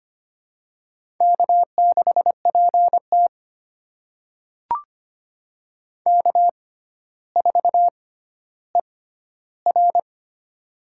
Morse code, heard as a keyed tone, about 25 wpm, spaced as for 13 wpm, 700 Hz. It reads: K6PT K4ER
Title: USA Call Signs - 4 Characters - 6x Character Spacing - 25wpm